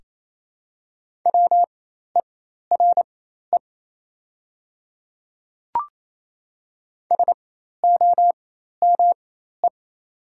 Morse code, heard as a keyed tone, about 28 wpm, spaced as for 13 wpm, 700 Hz. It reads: WERE SOME
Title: 4x Character Spacing - Top 100 Words - 28wpm